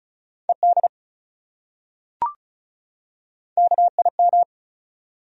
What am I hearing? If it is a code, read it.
Morse code, heard as a keyed tone, about 35 wpm, 700 Hz. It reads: ED KIM
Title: USA Common Names - 35wpm